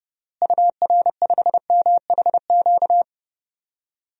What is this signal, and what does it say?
Morse code, heard as a keyed tone, about 30 wpm, 700 Hz. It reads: UR5MHQ
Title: DXCC-100 Call Signs - 6 Characters - Repeated 3x - 30wpm